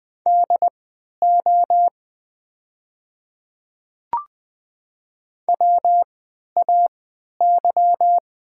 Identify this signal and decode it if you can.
Morse code, heard as a keyed tone, about 20 wpm, spaced as for 11 wpm, 700 Hz. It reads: DO WAY
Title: 3x Character Spacing - Top 200 Words - 20wpm